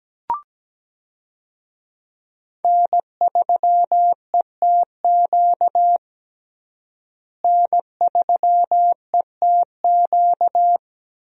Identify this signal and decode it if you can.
Morse code, heard as a keyed tone, about 17 wpm, 700 Hz. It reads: N3ETQ N3ETQ
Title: USA Call Signs - 5 Characters - Repeated 2x - 17wpm